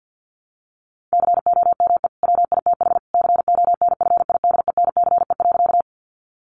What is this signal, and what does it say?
Morse code, heard as a keyed tone, about 50 wpm, 700 Hz. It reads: COME WITH CONFIDENCE.
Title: Sentences from Top 1500 Words 50wpm